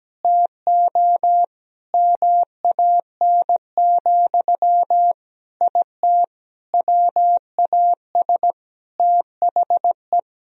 Morse code, heard as a keyed tone, about 17 wpm, 700 Hz. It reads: TO MAN, IT WAS THE
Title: Sentences from Top 1000 Words - Review Part 4 - 17wpm